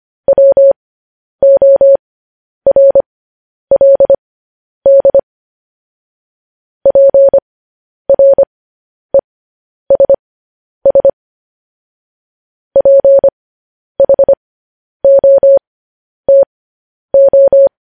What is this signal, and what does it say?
Morse code, heard as a keyed tone, about 25 wpm, spaced as for 10 wpm, 550 Hz. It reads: WORLD PRESS PHOTO
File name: podcast / News headlines-1x-25x10